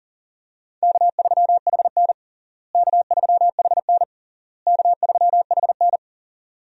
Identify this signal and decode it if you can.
Morse code, heard as a keyed tone, about 40 wpm, 700 Hz. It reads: K3HN K3HN K3HN
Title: USA Call Signs - 4 Characters - Repeated 3x - 40wpm